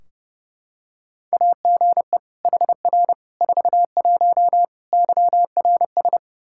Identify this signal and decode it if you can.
Morse code, heard as a keyed tone, about 30 wpm, 700 Hz. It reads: AGE HR 41 YRS
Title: Practice QSO - Exchange 3 - More Info - 30wpm